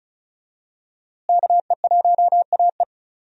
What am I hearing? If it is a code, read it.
Morse code, heard as a keyed tone, about 34 wpm, 700 Hz. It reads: KE1AE
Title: USA Call Signs - 5 Characters - Repeated 2x - 35wpm